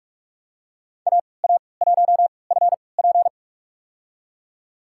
Morse code, heard as a keyed tone, about 46 wpm, spaced as for 25 wpm, 700 Hz. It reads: AA1FP
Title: USA Call Signs - 5 Characters - 3x Character Spacing - 45wpm